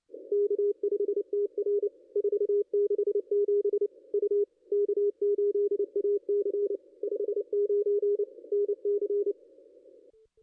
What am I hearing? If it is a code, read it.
Morse code, heard as a keyed tone, about 27 wpm, 410 Hz. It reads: K5TR 467 U K8AC 59 NC
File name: contests / ss-cw-k5tr-op-n5rz -0600